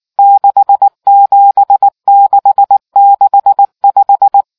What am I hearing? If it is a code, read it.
Morse code, heard as a keyed tone, about 19 wpm, 800 Hz. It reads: 67665